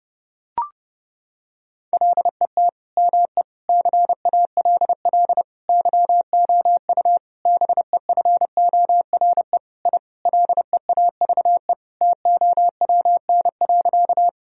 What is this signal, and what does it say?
Morse code, heard as a keyed tone, about 32 wpm, 700 Hz. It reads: LET ME CALL YOU BEFORE I LEAVE TOWN.
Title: Sentences from Top 1000 Words - Review Part 2 - 30wpm